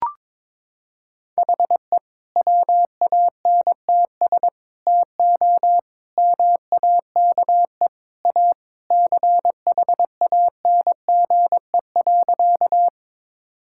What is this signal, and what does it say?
Morse code, heard as a keyed tone, about 22 wpm, 700 Hz. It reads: HE WANTS TO MAKE A CHANGE.
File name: practice / 22wpm / Sentences with 3-Word Phrases - 22wpm